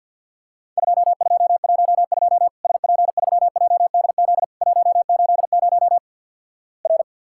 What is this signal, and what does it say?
Morse code, heard as a keyed tone, about 50 wpm, 700 Hz. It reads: 2212 SP3JDZ 180 R
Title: Worked All Europe DX Contest - QTC Exchange - 50wpm